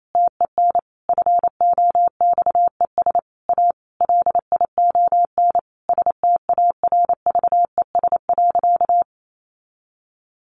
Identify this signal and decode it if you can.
Morse code, heard as a keyed tone, about 28 wpm, 700 Hz. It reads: TEN FOXES A LION STARVES.
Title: A Sound Of Thunder 28wpm